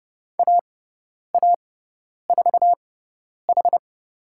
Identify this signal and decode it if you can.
Morse code, heard as a keyed tone, about 30 wpm, spaced as for 10 wpm, 700 Hz. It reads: AA4H
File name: Lesson 18g - Letter m - Call Signs 30wpm